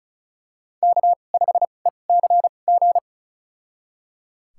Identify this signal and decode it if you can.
Morse code, heard as a keyed tone, about 34 wpm, spaced as for 25 wpm, 700 Hz. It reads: K5ECG
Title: USA Call Signs - 2x Character Spacing - 35wpm